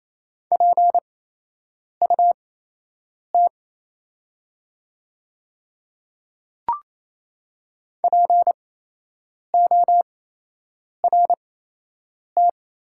Morse code, heard as a keyed tone, about 27 wpm, spaced as for 8 wpm, 690 Hz. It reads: PUT PORT
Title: 8x Character Spacing - Top 300 Words - 28wpm